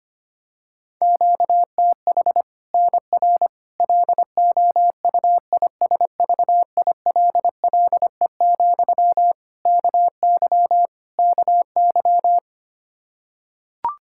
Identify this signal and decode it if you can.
Morse code, heard as a keyed tone, about 25 wpm, 700 Hz. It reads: QTH NR LOUISVILLE, KY KY
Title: K1USN SST QSO Practice 25wpm